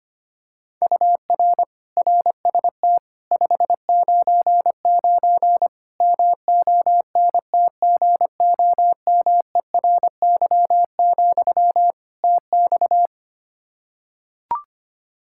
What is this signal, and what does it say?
Morse code, heard as a keyed tone, about 25 wpm, 700 Hz. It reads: UR RST 599 MONTGOMERY, TX
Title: K1USN SST QSO Practice 25wpm